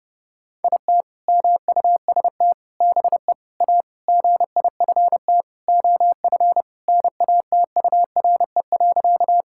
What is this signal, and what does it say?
Morse code, heard as a keyed tone, about 30 wpm, 700 Hz. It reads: IT MUST BE A GIFT OF NATURE.
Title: Sentences from Top 1000 Words - Review Part 6 - 30wpm